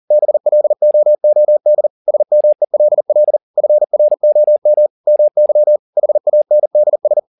Morse code, heard as a keyed tone, about 40 wpm, 600 Hz. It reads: BLOOD SMELL FROM MY HANDS
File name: the old man and the sea 2 40wpm